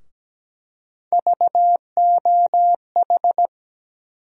Morse code, heard as a keyed tone, about 17 wpm, 700 Hz. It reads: VOH
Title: USA Call Sign Postfixes - 3 Letters 17wpm